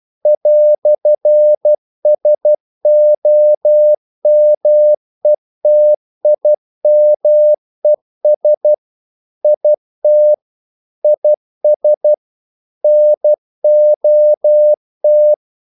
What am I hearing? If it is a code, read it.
Morse code, heard as a keyed tone, about 12 wpm, 600 Hz. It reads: "SOMETIMES IT IS NOT
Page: English quotations in CW (telegraphy - morse)